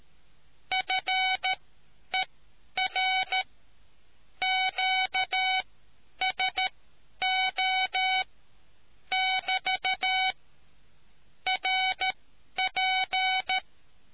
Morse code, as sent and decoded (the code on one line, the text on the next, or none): ..-. .-. / --.- ... --- / -...- / .-. .--.
FER QSO <BT> RP